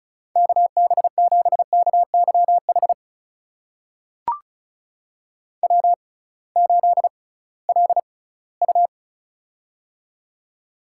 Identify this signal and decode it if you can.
Morse code, heard as a keyed tone, about 36 wpm, spaced as for 21 wpm, 700 Hz. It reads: KB7KYH W8LU
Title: USA Call Signs - 6x Character Spacing - 35wpm